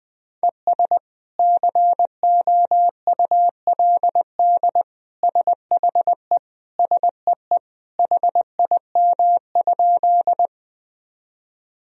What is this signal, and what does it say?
Morse code, heard as a keyed tone, about 20 wpm, 700 Hz. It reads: ES COULD SHE SEE HIM?
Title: Sentences from Top 100 Words and Abbreviations 20wpm